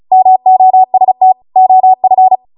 Morse code, heard as a keyed tone, about 36 wpm, 750 Hz. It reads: MOST OF
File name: Morse / 35WPM